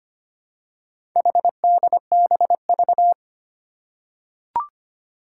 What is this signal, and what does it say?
Morse code, heard as a keyed tone, about 25 wpm, 700 Hz. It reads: HDBV
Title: Four Challenging Letter-Number Combinations 25wpm